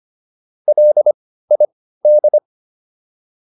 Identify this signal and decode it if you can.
Morse code, heard as a keyed tone, about 25 wpm, spaced as for 15 wpm, 600 Hz. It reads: LID